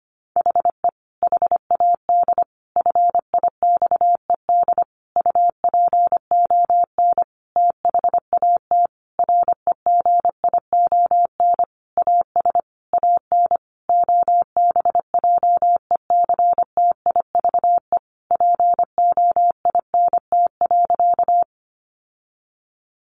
Morse code, heard as a keyed tone, about 25 wpm, 700 Hz. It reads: HE HAD FIXED UPON THAT REGION AS AN OBJECTIVE POINT.
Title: Sentences from Top 1500 Words 25wpm